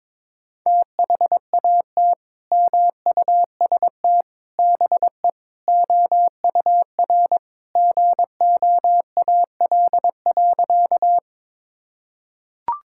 Morse code, heard as a keyed tone, about 21 wpm, 700 Hz. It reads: THAT MUST BE OUR GOAL.
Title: Sentences from Top 1000 Words - Review Part 5 - 22wpm